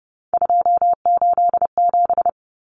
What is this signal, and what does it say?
Morse code, heard as a keyed tone, about 30 wpm, 700 Hz. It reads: 287
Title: Numbers - 3-Digit 30wpm